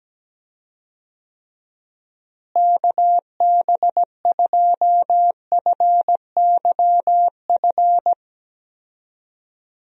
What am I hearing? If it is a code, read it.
Morse code, heard as a keyed tone, about 17 wpm, 700 Hz. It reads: KB2FYF
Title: USA Call Signs - 6 Characters - Repeated 3x - 17wpm